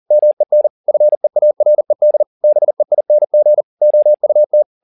Morse code, heard as a keyed tone, about 40 wpm, 600 Hz. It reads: MEN FEARED BEING OUT